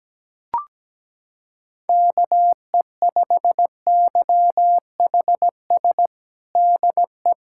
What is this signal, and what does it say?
Morse code, heard as a keyed tone, about 17 wpm, 700 Hz. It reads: KE5YHS DE
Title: USA Callsign DE Callsign - 6 Characters - 17wpm